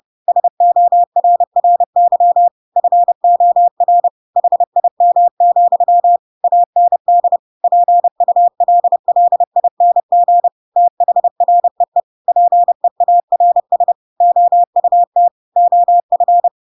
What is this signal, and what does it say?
Morse code, heard as a keyed tone, about 30 wpm, 700 Hz. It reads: SORRY FOR HIM, AND PULLING THREE PEARS OUT OF
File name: Pinocchio / Pinocchio ch7 30 30 ik1hge